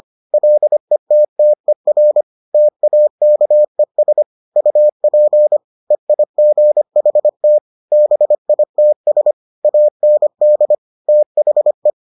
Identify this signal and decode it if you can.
Morse code, heard as a keyed tone, about 25 wpm, 600 Hz. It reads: LETTER TAKES UP EIGHT BITS AND THE